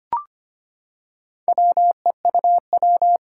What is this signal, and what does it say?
Morse code, heard as a keyed tone, about 25 wpm, 700 Hz. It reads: WEUW
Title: Four Challenging Letter-Number Combinations 25wpm